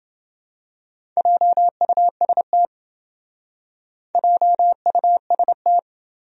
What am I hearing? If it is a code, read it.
Morse code, heard as a keyed tone, about 26 wpm, 700 Hz. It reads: JUST JUST